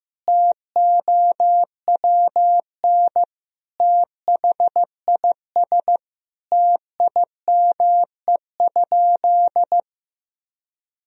Morse code, heard as a keyed tone, about 16 wpm, 700 Hz. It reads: TOWN THIS TIME?
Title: Sentences from Top 1000 Words - Review Part 2 - 15wpm